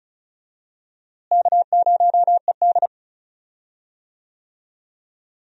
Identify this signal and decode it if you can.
Morse code, heard as a keyed tone, about 34 wpm, 700 Hz. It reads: K0ED
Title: USA Call Signs - 4 Characters - Repeated 3x - 35wpm